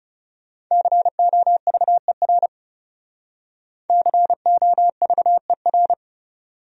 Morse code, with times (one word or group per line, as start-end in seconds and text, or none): 0.7-2.5 COVER
3.9-5.9 COVER